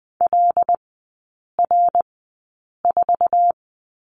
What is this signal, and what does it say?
Morse code, heard as a keyed tone, about 19 wpm, spaced as for 8 wpm, 700 Hz. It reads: LR4
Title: 074 Lesson 13e - Letter L - Three Characters 20wpm